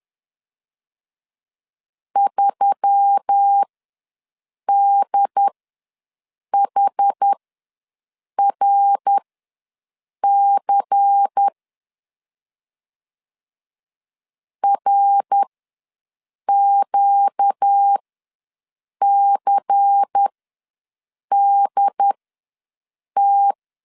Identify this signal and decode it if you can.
Morse code, heard as a keyed tone, about 10 wpm, spaced as for 6 wpm, 800 Hz. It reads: 3DHRC RQCDT
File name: LEZIONI / 7 GF9Z / 7 GF9 ws 10wpm 4034